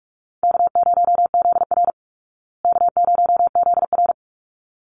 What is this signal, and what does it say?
Morse code, heard as a keyed tone, about 44 wpm, 700 Hz. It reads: K0ZR K0ZR
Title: Call Signs Repeated 3x - 45wpm